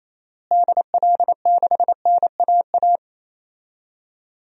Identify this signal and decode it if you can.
Morse code, heard as a keyed tone, about 28 wpm, 700 Hz. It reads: DL6NAA
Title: DXCC-100 Call Signs - 6 Characters - Repeated 3x - 28wpm